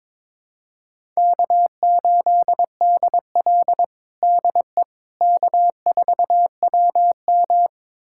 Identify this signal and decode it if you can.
Morse code, heard as a keyed tone, about 22 wpm, 700 Hz. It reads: K8DL DE K4WM
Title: USA Callsign DE Callsign - 4 Characters - 22wpm